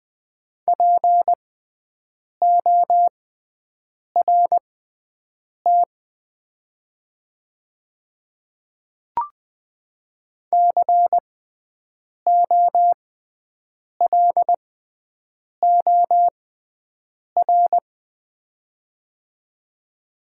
Morse code, heard as a keyed tone, about 20 wpm, spaced as for 7 wpm, 700 Hz. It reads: PORT COLOR
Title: 6x Character Spacing - Top 500 Words - 20wpm